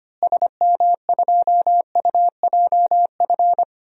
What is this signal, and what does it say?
Morse code, heard as a keyed tone, about 25 wpm, 700 Hz. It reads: SM2UJF
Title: DXCC-100 Call Signs - 6 Characters - Repeated 2x - 25wpm